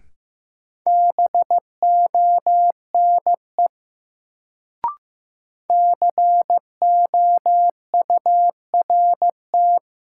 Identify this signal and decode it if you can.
Morse code, heard as a keyed tone, about 15 wpm, 700 Hz. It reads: BONE COURT
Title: Sentences from Top 1600 Words - New Word Drill - 15wpm